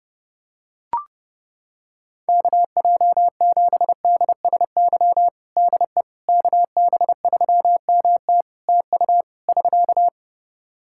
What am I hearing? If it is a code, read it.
Morse code, heard as a keyed tone, about 30 wpm, 700 Hz. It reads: KJ7DSY DE KB3MT TU <SK>
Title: K1USN SST QSO Practice 30wpm